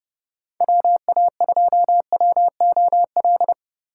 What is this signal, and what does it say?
Morse code, heard as a keyed tone, about 29 wpm, 700 Hz. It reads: WA2WOL